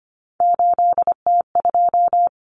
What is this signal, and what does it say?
Morse code, heard as a keyed tone, about 25 wpm, 700 Hz. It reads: 8T2
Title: Common Cut Numbers - 3-Digit 25wpm